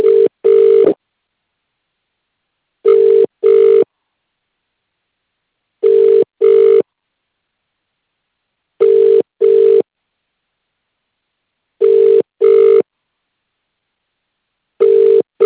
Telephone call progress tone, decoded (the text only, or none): ringback tone